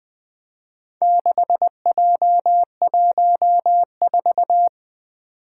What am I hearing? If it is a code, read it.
Morse code, heard as a keyed tone, about 21 wpm, 700 Hz. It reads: 6J14
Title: Four Challenging Letter-Number Combinations 20wpm